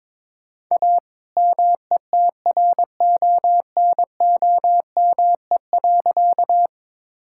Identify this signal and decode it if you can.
Morse code, heard as a keyed tone, about 22 wpm, 700 Hz. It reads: A METRONOME.
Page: morse-code-ninja-practice-22wpm